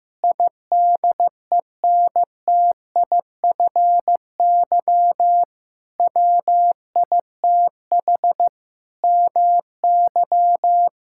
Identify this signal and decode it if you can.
Morse code, heard as a keyed tone, about 16 wpm, 700 Hz. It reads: IDENTIFY WITH MY
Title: Sentences from Top 1000 Words - Review Part 6 - 15wpm